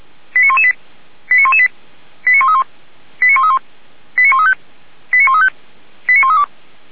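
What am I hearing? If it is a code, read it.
Five-tone selective call (ZVEI), heard as quick runs of five tones, tones of about 70 ms. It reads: 89118 89118 89131 89131 89135 89135 89132
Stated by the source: Five-tone ZVEI selective call 89103 received by SDR.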